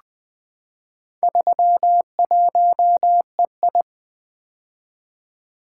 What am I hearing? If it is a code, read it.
Morse code, heard as a keyed tone, about 20 wpm, 700 Hz. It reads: 31EI
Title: Four Letter-Number Combinations - 20wpm